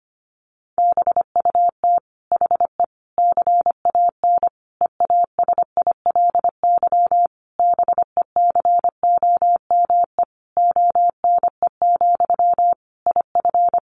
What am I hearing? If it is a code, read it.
Morse code, heard as a keyed tone, about 25 wpm, 700 Hz. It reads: BUT HE CAN EASILY BECOME ONE, IF
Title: Sentences from Top 900 Words 25wpm